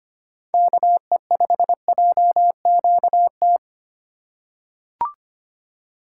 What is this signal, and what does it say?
Morse code, heard as a keyed tone, about 25 wpm, 700 Hz. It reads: KE5JQT
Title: USA Call Signs - 6 Characters - 4x Character Spacing - 25wpm